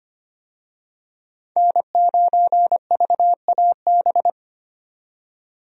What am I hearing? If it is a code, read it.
Morse code, heard as a keyed tone, about 25 wpm, 690 Hz. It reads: N9VAB
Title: USA Call Signs - 5 Characters - 25wpm